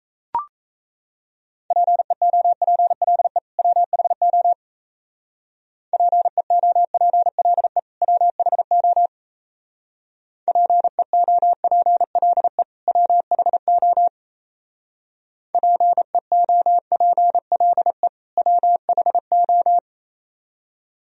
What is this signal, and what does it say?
Morse code, heard as a keyed tone, about 36 wpm, 700 Hz. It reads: PEOPLE WHO PEOPLE WHO PEOPLE WHO PEOPLE WHO